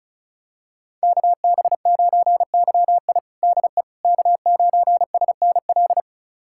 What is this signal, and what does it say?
Morse code, heard as a keyed tone, about 34 wpm, 700 Hz. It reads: KB9YI DE K9SNL
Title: USA Callsign DE Callsign - 5 Characters - 35wpm